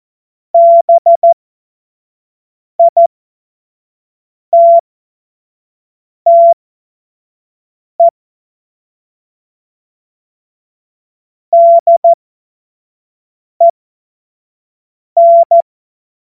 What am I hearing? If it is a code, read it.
Morse code, heard as a keyed tone, about 14 wpm, spaced as for 5 wpm, 680 Hz. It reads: BITTE DEN
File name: Morsekurs / Witze / Witz 14. September 2021, 25 F70